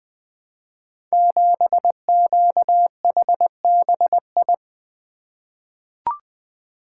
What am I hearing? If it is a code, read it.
Morse code, heard as a keyed tone, about 20 wpm, 700 Hz. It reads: 7QHBI
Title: Code Groups - 5 Letter:Numbers - 20wpm